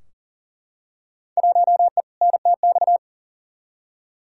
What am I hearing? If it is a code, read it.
Morse code, heard as a keyed tone, about 40 wpm, 700 Hz. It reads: 1E NTX
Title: Field Day Exchanges - 40wpm